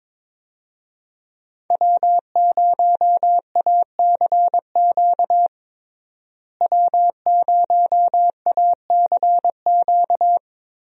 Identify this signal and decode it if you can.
Morse code, heard as a keyed tone, about 22 wpm, 700 Hz. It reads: W0ACQ W0ACQ
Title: USA Call Signs - 5 Characters - Repeated 3x - 22wpm